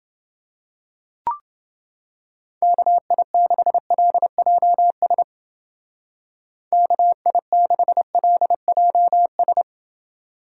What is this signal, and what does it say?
Morse code, heard as a keyed tone, about 29 wpm, 700 Hz. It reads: KI6LJS KI6LJS